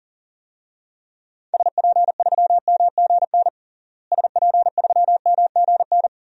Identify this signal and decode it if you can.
Morse code, heard as a keyed tone, about 40 wpm, 700 Hz. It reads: SP3MGN SP3MGN
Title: DXCC-100 Call Signs - 6 Characters - Repeated 3x - 40wpm